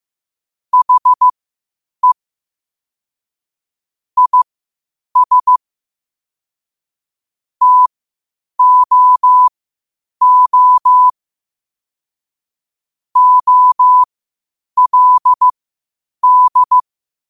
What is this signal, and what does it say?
Morse code, heard as a keyed tone, about 15 wpm, spaced as for 8 wpm, 1000 Hz. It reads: HE IS TOO OLD
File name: qotdfiles / 8wpm